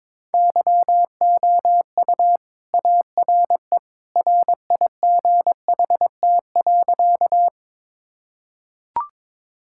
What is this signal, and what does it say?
Morse code, heard as a keyed tone, about 22 wpm, 700 Hz. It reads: YOU ARE RIGHT.
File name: practice / 22wpm / Sentences with 2-Word Phrases - 22wpm